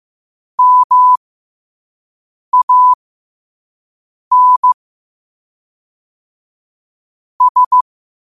Morse code, heard as a keyed tone, about 15 wpm, spaced as for 5 wpm, 1000 Hz. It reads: MAN S
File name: qotdfiles / 5wpm